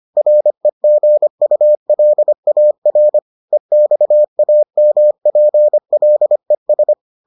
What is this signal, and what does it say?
Morse code, heard as a keyed tone, about 25 wpm, 600 Hz. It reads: REGULAR EXAMPLES